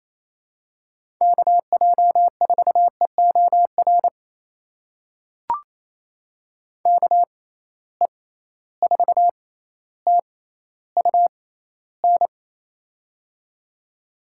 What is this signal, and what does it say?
Morse code, heard as a keyed tone, about 28 wpm, spaced as for 14 wpm, 700 Hz. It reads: KJ4EOR KE4TUN
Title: USA Call Signs - 6 Characters - 6x Character Spacing - 28wpm